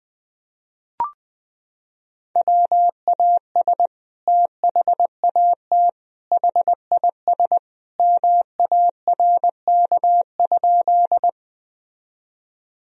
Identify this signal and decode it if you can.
Morse code, heard as a keyed tone, about 20 wpm, 700 Hz. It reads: WAS THAT HIS MARK?
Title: Sentences from Top 1000 Words - Review Part 1 - 20wpm